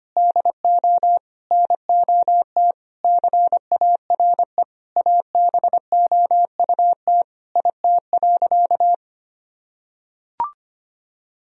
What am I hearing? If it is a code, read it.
Morse code, heard as a keyed tone, about 25 wpm, 700 Hz. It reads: DO NOT CARE ABOUT IT.
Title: Sentences from Top 1000 Words - Review Part 1 - 25wpm